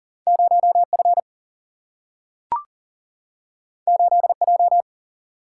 Morse code, heard as a keyed tone, about 40 wpm, 700 Hz. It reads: 0F 8J
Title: Single-Letter Single-Number 40wpm